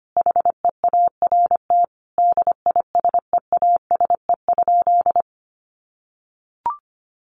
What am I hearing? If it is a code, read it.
Morse code, heard as a keyed tone, about 25 wpm, 700 Hz. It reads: HEART DISEASE?
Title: Sentences from Top 1200 Words 25wpm